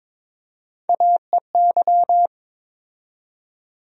Morse code, heard as a keyed tone, about 22 wpm, 700 Hz. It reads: AEY
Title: USA Call Sign Postfixes - 3 Letters 22wpm